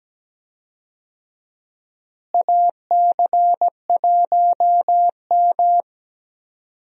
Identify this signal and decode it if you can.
Morse code, heard as a keyed tone, about 17 wpm, 700 Hz. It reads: AC1M
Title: USA Call Signs - 4 Characters - Repeated 3x - 17wpm